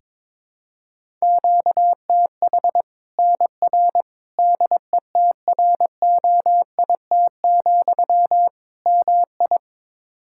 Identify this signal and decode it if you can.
Morse code, heard as a keyed tone, about 22 wpm, 700 Hz. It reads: QTH NR DETROIT, MI
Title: K1USN SST QSO Practice 22wpm